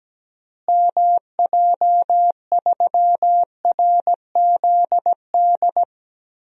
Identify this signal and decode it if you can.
Morse code, heard as a keyed tone, about 17 wpm, 700 Hz. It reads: MJ3RZD
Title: DXCC-100 Call Signs - 6 Characters - Repeated 3x - 17wpm